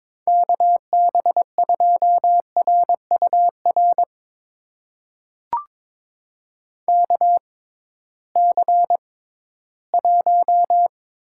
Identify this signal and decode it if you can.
Morse code, heard as a keyed tone, about 22 wpm, spaced as for 14 wpm, 700 Hz. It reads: KB2RUR KC1
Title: USA Call Signs - 6 Characters - 6x Character Spacing - 22wpm